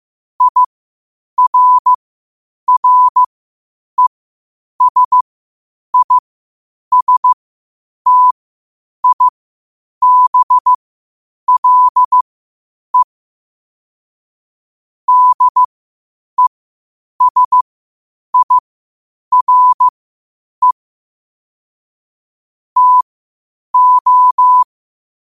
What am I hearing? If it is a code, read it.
Morse code, heard as a keyed tone, about 15 wpm, spaced as for 8 wpm, 1000 Hz. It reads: IRRESISTIBLE DESIRE TO
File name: qotdfiles / 8wpm